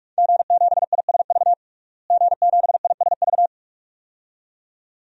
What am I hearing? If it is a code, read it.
Morse code, heard as a keyed tone, about 45 wpm, 700 Hz. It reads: G7ISV G7ISV
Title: DXCC-100 Call Signs - 5 Characters - Repeated 2x - 45wpm